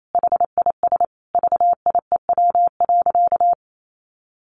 Morse code, heard as a keyed tone, about 28 wpm, 700 Hz. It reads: HIS VIEW.
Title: Sentences from Top 1400 Words 28wpm